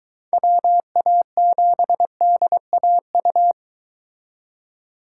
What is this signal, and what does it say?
Morse code, heard as a keyed tone, about 22 wpm, 700 Hz. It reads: WA7DAU